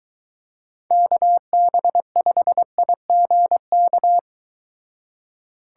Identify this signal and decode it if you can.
Morse code, heard as a keyed tone, about 23 wpm, 700 Hz. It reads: KB5IGK